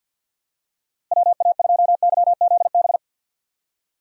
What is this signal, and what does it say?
Morse code, heard as a keyed tone, about 46 wpm, 700 Hz. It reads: WA2YZB
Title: USA Call Signs - 6 Characters - Repeated 3x - 50wpm